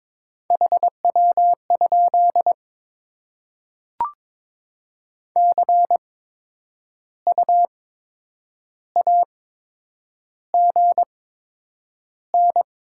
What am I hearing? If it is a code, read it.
Morse code, heard as a keyed tone, about 22 wpm, spaced as for 8 wpm, 700 Hz. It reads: HW? CU AGN
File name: Top 20 QSO Elements - 8x Character Spacing - 22wpm